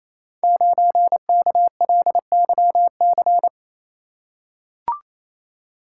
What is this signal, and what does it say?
Morse code, heard as a keyed tone, about 29 wpm, 700 Hz. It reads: 9KLYC
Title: Code Groups - 5 Letter:Numbers - 28wpm